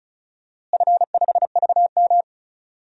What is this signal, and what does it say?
Morse code, heard as a keyed tone, about 36 wpm, 690 Hz. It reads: F5VM
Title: DXCC-100 Call Signs - 4 Characters - Repeated 3x - 35wpm